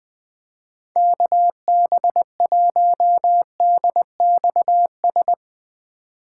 Morse code, heard as a keyed tone, about 20 wpm, 700 Hz. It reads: KB1DXS